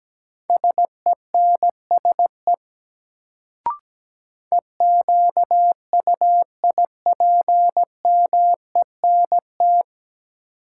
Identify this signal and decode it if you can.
Morse code, heard as a keyed tone, about 17 wpm, 700 Hz. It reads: SENSE EQUIPMENT
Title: Sentences from Top 700 Words - New Word Drill - 17wpm